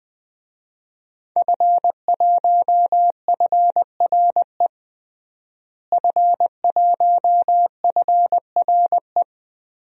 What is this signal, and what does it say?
Morse code, heard as a keyed tone, about 21 wpm, 700 Hz. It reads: F1FRE F1FRE
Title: DXCC-100 Call Signs - 5 Characters - Repeated 3x - 20wpm